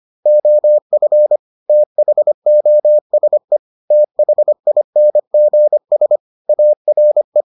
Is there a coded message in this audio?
Yes: Morse code from 0.3 to 7.4 s, heard as a keyed tone, about 25 wpm, 600 Hz. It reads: OF THOSE THINGS ARE